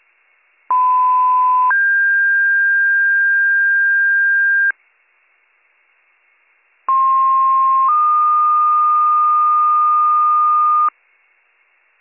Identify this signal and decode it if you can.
Two-tone page, heard as a one-second tone, then a three-second tone, about 1010 then 1600 Hz, about 1060 then 1220 Hz. It is heard from 0.7 to 10.9 s.